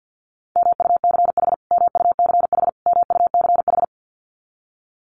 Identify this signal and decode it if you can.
Morse code, heard as a keyed tone, about 48 wpm, 700 Hz. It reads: MUCH MUCH MUCH
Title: Top 200 Words Repeated 50wpm